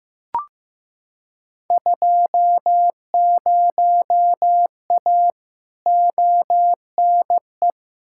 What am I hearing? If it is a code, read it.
Morse code, heard as a keyed tone, about 15 wpm, 700 Hz. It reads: 20A ONE
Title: Field Day Exchanges Encore - 15wpm